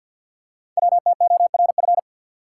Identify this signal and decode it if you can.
Morse code, heard as a keyed tone, about 46 wpm, 690 Hz. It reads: WTORF
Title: Code Groups - 5 Letter:Numbers - 50wpm